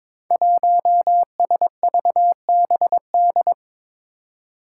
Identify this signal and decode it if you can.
Morse code, heard as a keyed tone, about 22 wpm, 700 Hz. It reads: 1SVBD
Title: Five Challenging Letter-Number Combinations 22wpm